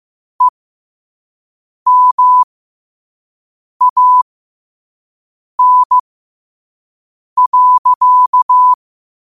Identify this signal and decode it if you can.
Morse code, heard as a keyed tone, about 15 wpm, spaced as for 5 wpm, 1000 Hz. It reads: EMAN.
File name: qotdfiles / 5wpm